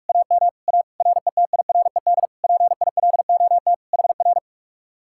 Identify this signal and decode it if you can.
Morse code, heard as a keyed tone, about 57 wpm, 700 Hz. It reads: AM A RETIRED PILOT HR